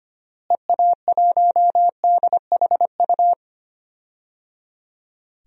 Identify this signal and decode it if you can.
Morse code, heard as a keyed tone, about 25 wpm, 700 Hz. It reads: EA1DHU
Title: DXCC-100 Call Signs - 6 Characters - Repeated 3x - 25wpm